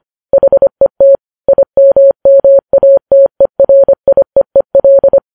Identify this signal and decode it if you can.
Morse code, heard as a keyed tone, about 25 wpm, 550 Hz. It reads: HET IMMATERIEEL